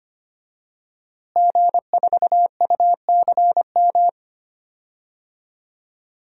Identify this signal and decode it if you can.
Morse code, heard as a keyed tone, about 25 wpm, 700 Hz. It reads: G4UCM
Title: DXCC-100 Call Signs - 5 Characters - Repeated 2x - 25wpm